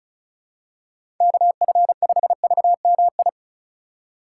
Morse code, heard as a keyed tone, about 36 wpm, 700 Hz. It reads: KF5VMI
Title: USA Call Signs - 6 Characters - 3x Character Spacing - 35wpm